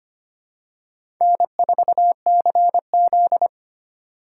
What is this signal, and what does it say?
Morse code, heard as a keyed tone, about 25 wpm, 700 Hz. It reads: N4CZ
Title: USA Call Signs - 4 Characters - 3x Character Spacing - 25wpm